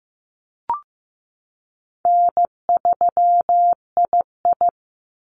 Morse code, heard as a keyed tone, about 15 wpm, 700 Hz. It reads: N3II
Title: NA SOTA Activator Callsigns 15wpm